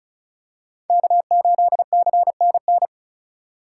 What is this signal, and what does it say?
Morse code, heard as a keyed tone, about 36 wpm, 700 Hz. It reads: K8CNN
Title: USA Call Signs - 3x Character Spacing - 35wpm